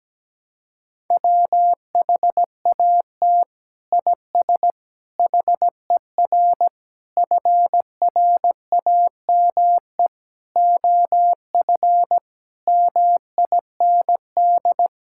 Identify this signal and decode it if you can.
Morse code, heard as a keyed tone, about 17 wpm, 700 Hz. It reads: WHAT IS HER FRAME OF MIND
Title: Sentences from Top 1000 Words - Review Part 6 - 17wpm